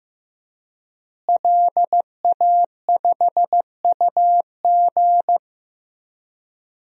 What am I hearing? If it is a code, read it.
Morse code, heard as a keyed tone, about 15 wpm, 700 Hz. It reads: LA5UG
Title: DXCC-100 Call Signs - 5 Characters - Repeated 2x - 15wpm